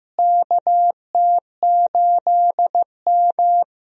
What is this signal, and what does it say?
Morse code, heard as a keyed tone, about 15 wpm, 700 Hz. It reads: KT8M
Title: USA Call Signs - 4 Characters - Repeated 3x - 15wpm